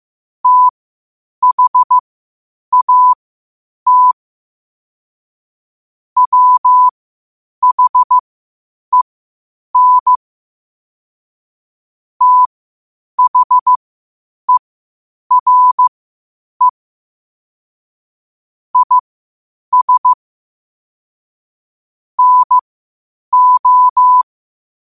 Morse code, heard as a keyed tone, about 15 wpm, spaced as for 8 wpm, 1000 Hz. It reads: THAT WHEN THERE IS NO
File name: qotdfiles / 8wpm